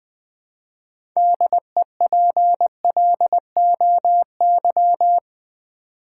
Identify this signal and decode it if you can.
Morse code, heard as a keyed tone, about 20 wpm, 700 Hz. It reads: DEPLOY
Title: 6 Letter Words - First 3 Letters Given - 20wpm